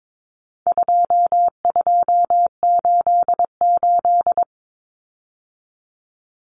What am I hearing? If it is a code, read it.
Morse code, heard as a keyed tone, about 22 wpm, 700 Hz. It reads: 2288
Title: Common Cut Numbers - 4-Digit 22wpm